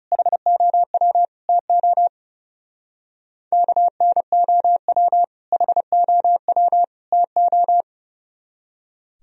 Morse code, heard as a keyed tone, about 37 wpm, 700 Hz. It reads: HOW TO KNOW HOW TO
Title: Speed-Racing - Sets of 3 Words 30wpm